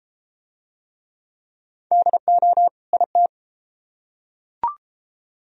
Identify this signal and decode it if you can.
Morse code, heard as a keyed tone, about 33 wpm, 700 Hz. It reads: DO IT